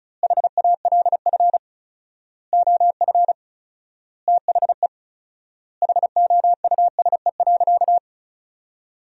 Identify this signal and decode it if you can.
Morse code, heard as a keyed tone, about 34 wpm, 700 Hz. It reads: HALF OF THE HOUSE.
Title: 4x Word-Spacing - Sentences from Top 400 Words - 35wpm